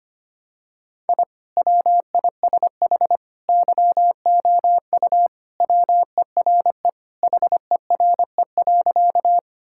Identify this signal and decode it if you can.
Morse code, heard as a keyed tone, about 25 wpm, 700 Hz. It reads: I WISH YOU WERE HERE.
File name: practice / 25wpm / Sentences with 2-Word Phrases - 25wpm